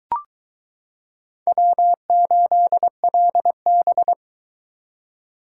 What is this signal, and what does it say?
Morse code, heard as a keyed tone, about 22 wpm, 700 Hz. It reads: W8LB